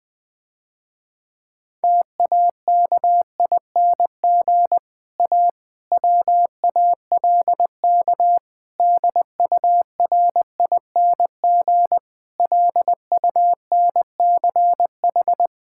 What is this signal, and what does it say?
Morse code, heard as a keyed tone, about 20 wpm, 700 Hz. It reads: TAKING A WALK DURING LUNCH
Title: Difficult Sentences with Hints - 20wpm